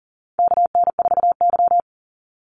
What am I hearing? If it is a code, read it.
Morse code, heard as a keyed tone, about 40 wpm, 700 Hz. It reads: KN4Y
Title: Call Signs Repeated 3x - 40wpm